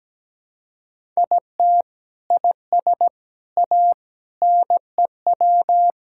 Morse code, heard as a keyed tone, about 17 wpm, 700 Hz. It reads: IT IS A NEW
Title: Sentences from Top 1000 Words - Review Part 6 - 17wpm